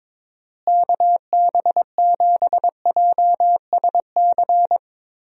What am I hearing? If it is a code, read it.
Morse code, heard as a keyed tone, about 22 wpm, 700 Hz. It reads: KB7JSC
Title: USA Call Signs - 6 Characters - Repeated 2x - 22wpm